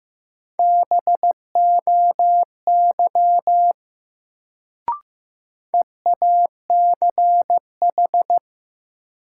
Morse code, heard as a keyed tone, about 15 wpm, 700 Hz. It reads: BOY EACH
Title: Sentences from Top 200 Words - New Word Drill - 15wpm